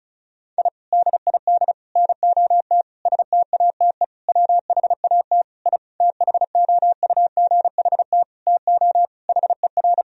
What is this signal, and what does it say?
Morse code, heard as a keyed tone, about 34 wpm, 700 Hz. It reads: I DID NOT STATE WHAT I THOUGHT TO HER
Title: Sentences from Top 1000 Words - Review Part 2 - 35wpm